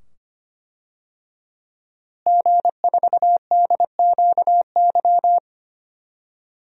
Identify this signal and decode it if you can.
Morse code, heard as a keyed tone, about 25 wpm, 700 Hz. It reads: G4DQY
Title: DXCC-100 Call Signs - 5 Characters - Repeated 2x - 25wpm